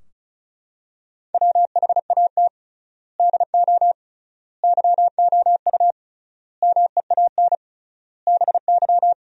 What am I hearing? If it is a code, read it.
Morse code, heard as a keyed tone, about 34 wpm, 700 Hz. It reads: WHAT DO YOU MEAN BY
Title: 3x Word-Spacing - Sentences from Top 400 Words - 35wpm